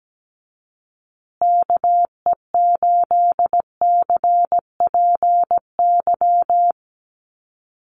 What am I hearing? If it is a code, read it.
Morse code, heard as a keyed tone, about 17 wpm, 700 Hz. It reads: KE8CPY
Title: USA Call Signs Repeated 3x 17wpm